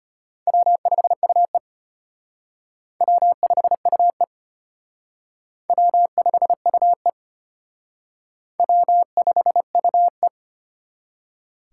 Morse code, heard as a keyed tone, about 31 wpm, 700 Hz. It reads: W5UE W5UE W5UE W5UE
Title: Speed Racing - USA 1x2 Call Signs 25wpm